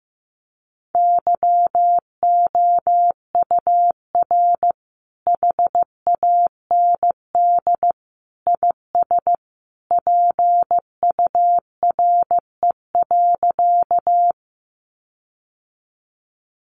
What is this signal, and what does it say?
Morse code, heard as a keyed tone, about 15 wpm, 700 Hz. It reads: YOUR HAND IS PURE.
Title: Sentences from Top 1500 Words 15wpm